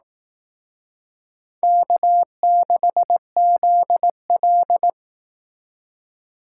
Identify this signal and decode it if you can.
Morse code, heard as a keyed tone, about 18 wpm, 700 Hz. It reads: K6ZL
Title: Speed Racing - USA 1x2 Call Signs 15wpm